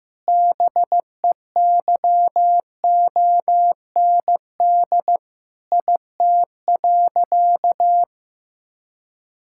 Morse code, heard as a keyed tone, about 15 wpm, 700 Hz. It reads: BEYOND IT.
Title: Sentences from Top 1000 Words - Review Part 6 - 15wpm